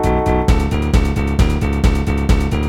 There is no code.